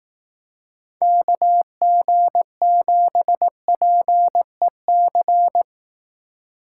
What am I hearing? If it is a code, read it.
Morse code, heard as a keyed tone, about 18 wpm, 700 Hz. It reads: KG7PEC
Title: Speed Racing - USA Call Signs 15wpm